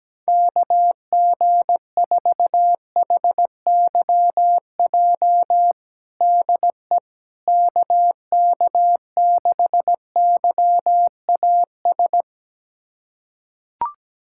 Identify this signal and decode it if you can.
Morse code, heard as a keyed tone, about 17 wpm, 700 Hz. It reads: KG4HYJ DE KK6YAS
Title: USA Callsign DE Callsign - 6 Characters - 17wpm